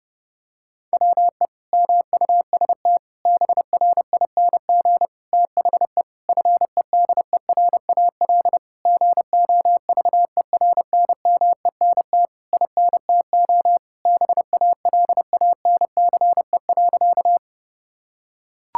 Morse code, heard as a keyed tone, about 30 wpm, 700 Hz. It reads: WE MUST BRING THE FEDERAL GOVERNMENT INTO BALANCE.
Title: Sentences from Top 1000 Words - Review Part 6 - 30wpm